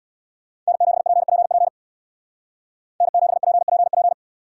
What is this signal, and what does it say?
Morse code, heard as a keyed tone, about 70 wpm, 700 Hz. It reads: N7PPP N7PPP